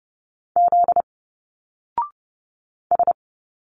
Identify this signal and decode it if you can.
Morse code, heard as a keyed tone, about 28 wpm, 700 Hz. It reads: Z S
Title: Single Letters 30wpm